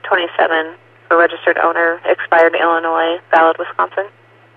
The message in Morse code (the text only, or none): none